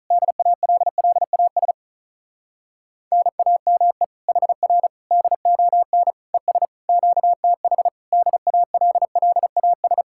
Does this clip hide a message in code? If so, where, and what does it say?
Yes: Morse code from 0.1 to 10.0 s, heard as a keyed tone, about 44 wpm, 700 Hz. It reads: DALLAS NAME HR DON ES QTH DALLAS